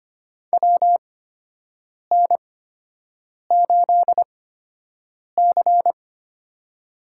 Morse code, heard as a keyed tone, about 25 wpm, spaced as for 7 wpm, 700 Hz. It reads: WN8C
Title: USA Call Signs - 4 Characters - 8x Character Spacing - 25wpm